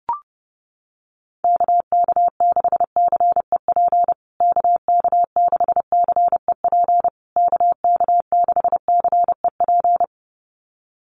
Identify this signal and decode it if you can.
Morse code, heard as a keyed tone, about 30 wpm, 700 Hz. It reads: KK6CEP KK6CEP KK6CEP
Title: USA Call Signs Repeated 3x 30wpm